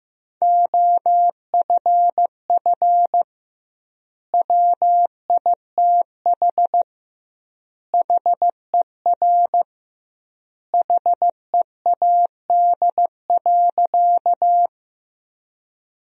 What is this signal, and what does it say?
Morse code, heard as a keyed tone, about 16 wpm, 700 Hz. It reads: OFF WITH HER HEAD.
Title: 2x Word-Spacing - Sentences from Top 300 Words - 15wpm